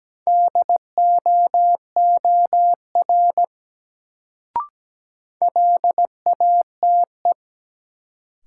Morse code, heard as a keyed tone, about 17 wpm, 700 Hz. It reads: DOOR LATE
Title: Sentences from Top 300 Words - New Word Drill - 17wpm